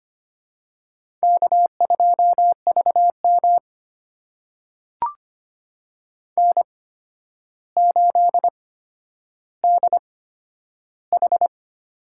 Morse code, heard as a keyed tone, about 25 wpm, spaced as for 11 wpm, 700 Hz. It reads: K2VM N8DH
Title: USA Call Signs - 4 Characters - 8x Character Spacing - 25wpm